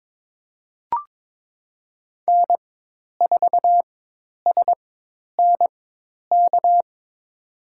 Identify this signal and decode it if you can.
Morse code, heard as a keyed tone, about 22 wpm, spaced as for 10 wpm, 700 Hz. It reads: N4SNK
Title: USA Call Signs - 5 Characters - 4x Character Spacing - 22wpm